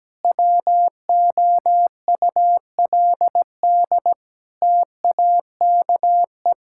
Morse code, heard as a keyed tone, about 17 wpm, 700 Hz. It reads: WOULD TAKE
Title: Sentences from Top 1600 Words 17wpm